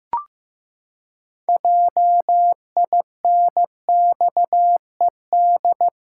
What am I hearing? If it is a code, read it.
Morse code, heard as a keyed tone, about 15 wpm, 700 Hz. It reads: JINXED
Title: Pangram Challenge v2 - 15wpm